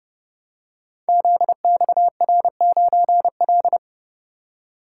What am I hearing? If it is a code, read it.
Morse code, heard as a keyed tone, about 30 wpm, 700 Hz. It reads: ZXR9L
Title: Code Groups - 5 Letter:Numbers - 30wpm